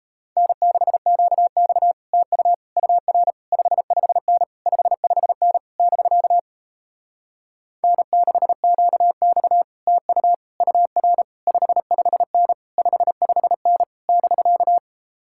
Morse code, heard as a keyed tone, about 38 wpm, 700 Hz. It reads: N6QX TU UR 55N 55N <BK> N6QX TU UR 55N 55N <BK>